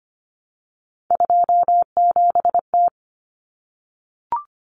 Morse code, heard as a keyed tone, about 25 wpm, 700 Hz. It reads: 27T
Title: Common Cut Numbers - 3-Digit 25wpm